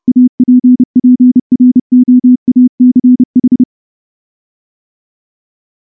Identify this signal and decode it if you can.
Morse code, heard as a keyed tone, about 28 wpm, 250 Hz. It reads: APPROACH
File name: morse / morse-training-30WPM